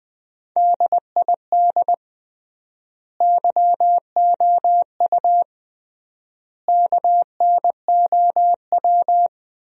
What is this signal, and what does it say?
Morse code, heard as a keyed tone, about 21 wpm, 700 Hz. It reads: DID YOU KNOW
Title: 3x Word-Spacing - Sentences from Top 100 Words - 20wpm